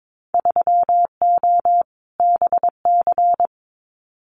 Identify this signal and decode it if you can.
Morse code, heard as a keyed tone, about 22 wpm, 700 Hz. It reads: 3O BC
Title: Winter Field Day Exchanges Encore 22wpm